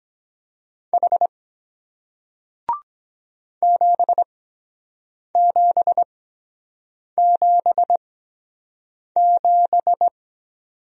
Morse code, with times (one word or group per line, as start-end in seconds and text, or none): 0.9-1.3 H
3.6-10.1 7777